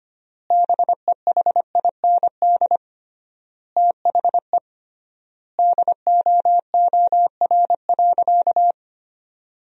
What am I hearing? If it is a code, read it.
Morse code, heard as a keyed tone, about 25 wpm, 700 Hz. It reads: BEHIND THE DOOR.
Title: 3x Word-Spacing - Sentences from Top 500 Words - 25wpm